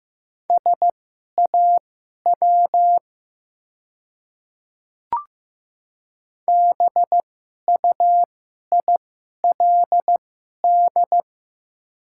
Morse code, heard as a keyed tone, about 15 wpm, spaced as for 11 wpm, 700 Hz. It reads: SAW BUILD
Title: 2x Character Spacing - Top 300 Words - 15wpm